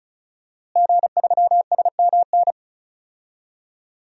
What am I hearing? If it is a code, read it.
Morse code, heard as a keyed tone, about 36 wpm, 700 Hz. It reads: G3SMN
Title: DXCC-100 Call Signs - 5 Characters - Repeated 3x - 35wpm